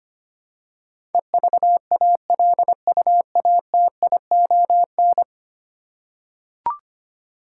Morse code, heard as a keyed tone, about 25 wpm, 700 Hz. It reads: EVALUATION
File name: 10 Letter Words - First 3 Letters Given - 25wpm